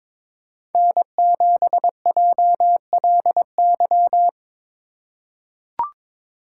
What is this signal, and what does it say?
Morse code, heard as a keyed tone, about 22 wpm, 700 Hz. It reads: N7JLY
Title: USA Call Signs - 3x Character Spacing - 22wpm